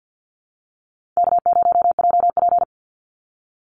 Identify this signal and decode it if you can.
Morse code, heard as a keyed tone, about 50 wpm, 700 Hz. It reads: K0JP
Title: Easy Call Signs - 50wpm